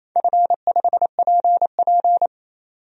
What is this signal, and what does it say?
Morse code, heard as a keyed tone, about 28 wpm, 700 Hz. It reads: F5PP
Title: DXCC-100 Call Signs - 4 Characters - 28wpm